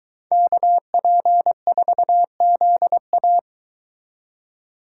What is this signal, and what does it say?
Morse code, heard as a keyed tone, about 22 wpm, 700 Hz. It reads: KP4ZA